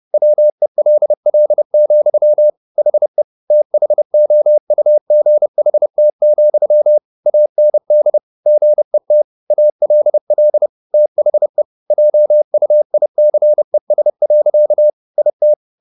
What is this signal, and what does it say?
Morse code, heard as a keyed tone, about 30 wpm, 600 Hz. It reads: WELL, HE THOUGHT, AND GET ALL THE JUICES. IT
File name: the old man and the sea / the old man and the sea 1 30wpm